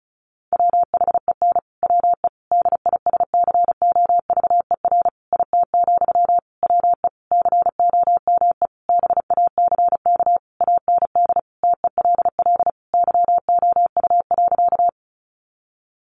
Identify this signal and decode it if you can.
Morse code, heard as a keyed tone, about 34 wpm, 700 Hz. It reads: WHEN WE DISCOVER IT, WE COME BACK AND TELL YOU.
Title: Sentences from Top 1100 Words 35wpm